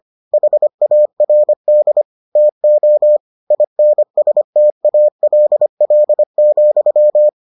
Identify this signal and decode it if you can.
Morse code, heard as a keyed tone, about 25 wpm, 600 Hz. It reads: HARD TO INSTALL,